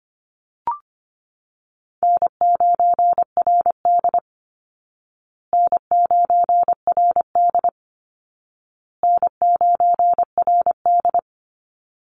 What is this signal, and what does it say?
Morse code, heard as a keyed tone, about 25 wpm, 700 Hz. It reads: N9RD N9RD N9RD
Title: Easy Call Signs Repeated 3x - 25wpm